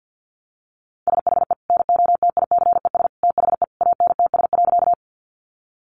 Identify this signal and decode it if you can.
Morse code, heard as a keyed tone, about 50 wpm, 700 Hz. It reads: SHE NOTICES THE ANTS.
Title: Easy Phrases 50wpm